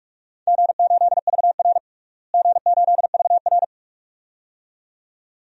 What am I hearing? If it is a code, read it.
Morse code, heard as a keyed tone, about 43 wpm, 700 Hz. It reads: G8VR G8VR
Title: DXCC-100 Call Signs - 4 Characters - Repeated 2x - 45wpm